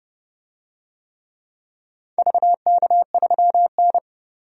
Morse code, heard as a keyed tone, about 28 wpm, 700 Hz. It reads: VK3N
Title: DXCC-100 Call Signs - 4 Characters - 30wpm